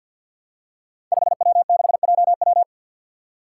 Morse code, heard as a keyed tone, about 52 wpm, 700 Hz. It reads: 5W6JW
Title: Five Challenging Letter-Number Combinations 50wpm